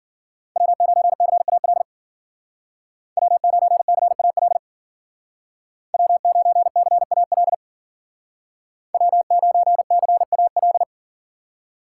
Morse code, heard as a keyed tone, about 52 wpm, 700 Hz. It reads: W9CAL W9CAL W9CAL W9CAL